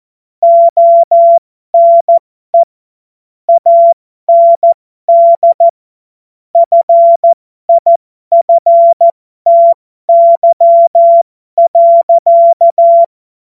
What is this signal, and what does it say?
Morse code, heard as a keyed tone, about 14 wpm, spaced as for 12 wpm, 680 Hz. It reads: ONE AND FIFTY.
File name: Morsekurs / Witze